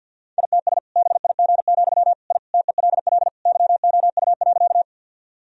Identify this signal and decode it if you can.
Morse code, heard as a keyed tone, about 52 wpm, 700 Hz. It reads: ITS BIG, I TELL YOU.
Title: Sentences from Top 1000 Words - Review Part 1 - 50wpm